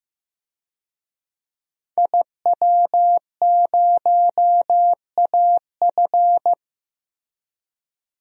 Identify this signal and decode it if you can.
Morse code, heard as a keyed tone, about 15 wpm, 700 Hz. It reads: IW0AF
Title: DXCC-100 Call Signs - 5 Characters - Repeated 2x - 15wpm